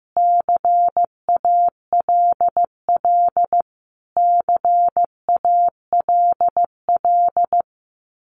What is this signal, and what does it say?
Morse code, heard as a keyed tone, about 15 wpm, 700 Hz. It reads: CALL CALL
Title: Top 200 Words Repeated 15wpm